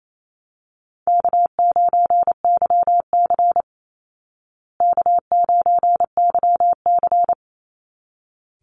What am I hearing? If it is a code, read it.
Morse code, heard as a keyed tone, about 28 wpm, 700 Hz. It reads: K9YC K9YC
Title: Call Signs Repeated 3x - 28wpm